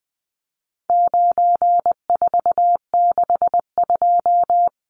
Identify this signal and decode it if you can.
Morse code, heard as a keyed tone, about 20 wpm, 700 Hz. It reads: 9462
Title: Numbers - 4-Digit 20wpm